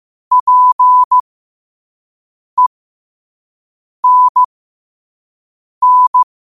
Morse code, heard as a keyed tone, about 15 wpm, spaced as for 5 wpm, 1000 Hz. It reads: PENN
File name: qotdfiles / 5wpm